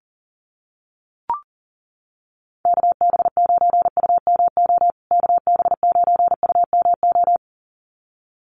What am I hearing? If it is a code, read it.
Morse code, heard as a keyed tone, about 38 wpm, 700 Hz. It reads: KB9UMO KB9UMO
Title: USA Call Signs Repeated 2x 40wpm